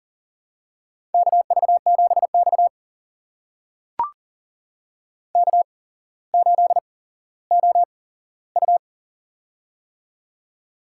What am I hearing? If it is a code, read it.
Morse code, heard as a keyed tone, about 38 wpm, spaced as for 17 wpm, 700 Hz. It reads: KV7X K8OU
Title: USA Call Signs - 8x Character Spacing - 40wpm